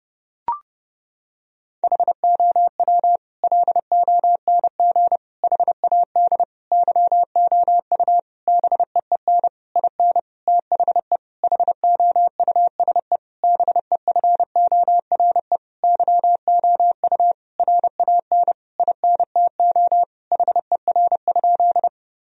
Morse code, heard as a keyed tone, about 30 wpm, 700 Hz. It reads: HOW LONG HAD YOU BEEN IN THE HOUSE BEFORE YOU RAN INTO HER?
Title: Sentences from Top 1000 Words - Review Part 3 - 30wpm